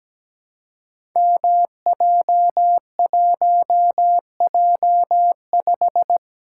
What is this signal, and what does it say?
Morse code, heard as a keyed tone, about 17 wpm, 700 Hz. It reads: MJ1J5
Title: Code Groups - 5 Letter:Numbers - 17wpm